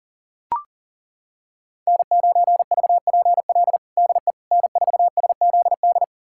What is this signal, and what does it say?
Morse code, heard as a keyed tone, about 40 wpm, 700 Hz. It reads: N9VPL DE N4SZD
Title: USA Callsign DE Callsign - 5 Characters - 40wpm